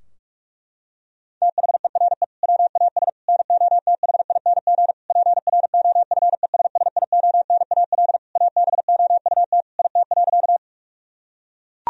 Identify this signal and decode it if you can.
Morse code, heard as a keyed tone, about 41 wpm, 700 Hz. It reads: THERE WAS NOTHING PROFESSIONAL ABOUT IT.
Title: Sentences from Top 1000 Words - Review Part 4 - 45wpm